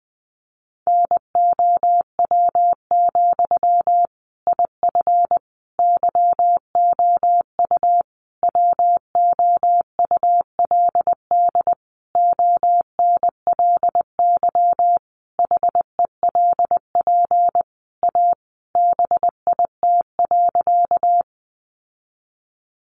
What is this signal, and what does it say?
Morse code, heard as a keyed tone, about 20 wpm, 700 Hz. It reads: NOW, IF YOU WOULD ONLY HELP A BIT.
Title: Sentences from Top 900 Words 20wpm